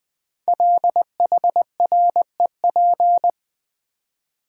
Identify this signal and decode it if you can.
Morse code, heard as a keyed tone, about 20 wpm, 700 Hz. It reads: LHREP
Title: Code Groups - 5 Letters - 20wpm